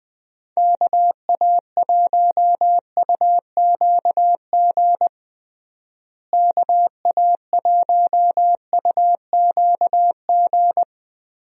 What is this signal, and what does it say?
Morse code, heard as a keyed tone, about 20 wpm, 700 Hz. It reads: KA1UQG KA1UQG
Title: USA Call Signs - 6 Characters - Repeated 3x - 20wpm